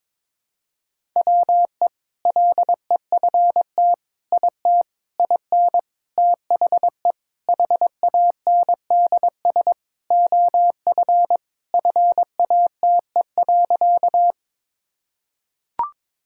Morse code, heard as a keyed tone, about 22 wpm, 700 Hz. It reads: WE LEFT IT IN THE HANDS OF FATE.
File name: practice / 22wpm / Sentences with 4-Word Phrases - 22wpm